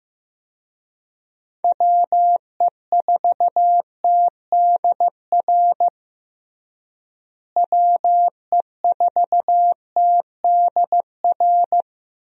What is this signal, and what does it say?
Morse code, heard as a keyed tone, about 15 wpm, 700 Hz. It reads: WE4TDR WE4TDR
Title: USA Call Signs - 6 Characters - Repeated 3x - 15wpm